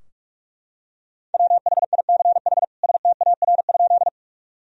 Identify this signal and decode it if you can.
Morse code, heard as a keyed tone, about 44 wpm, 700 Hz. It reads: WHICH STAR?
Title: Sentences from Top 1000 Words - Review Part 2 - 45wpm